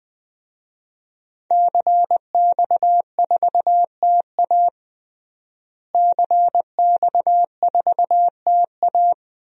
Morse code, heard as a keyed tone, about 20 wpm, 700 Hz. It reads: CX4TA CX4TA
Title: DXCC-100 Call Signs - 5 Characters - Repeated 3x - 20wpm